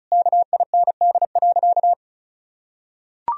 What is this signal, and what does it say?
Morse code, heard as a keyed tone, about 34 wpm, 700 Hz. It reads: KIND.
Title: Sentences from Top 1000 Words - Review Part 4 - 35wpm